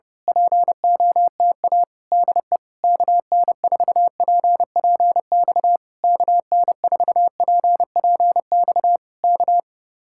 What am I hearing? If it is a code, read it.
Morse code, heard as a keyed tone, about 26 wpm, 700 Hz. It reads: POTA DE KN4PPX KN4PPX K
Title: Speed-Racing Common QSO Phrases - 25wpm